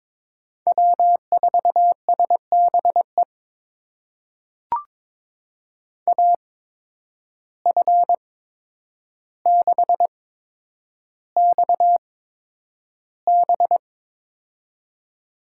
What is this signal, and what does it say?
Morse code, heard as a keyed tone, about 21 wpm, spaced as for 9 wpm, 700 Hz. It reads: W4SBE AF6XB
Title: USA Call Signs - 5 Characters - 8x Character Spacing - 22wpm